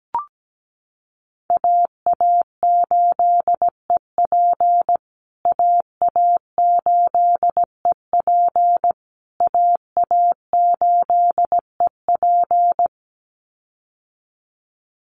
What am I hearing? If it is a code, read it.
Morse code, heard as a keyed tone, about 17 wpm, 700 Hz. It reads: AA8EP AA8EP AA8EP
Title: USA Call Signs Repeated 3x 17wpm